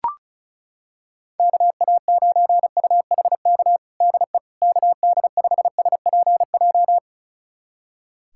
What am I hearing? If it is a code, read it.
Morse code, heard as a keyed tone, about 34 wpm, 700 Hz. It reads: KA9UHK DE KD5SPJ
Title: USA Callsign DE Callsign - 6 Characters - 35wpm